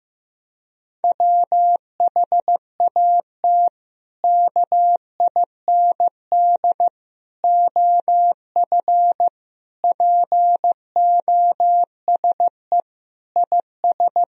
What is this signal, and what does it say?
Morse code, heard as a keyed tone, about 15 wpm, 700 Hz. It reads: WHAT KIND OF POSE IS
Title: Sentences from Top 1000 Words - Review Part 2 - 15wpm